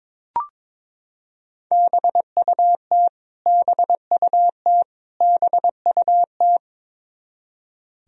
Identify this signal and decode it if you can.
Morse code, heard as a keyed tone, about 22 wpm, 700 Hz. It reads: BUT BUT BUT
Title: Top 100 Words Repeated 22wpm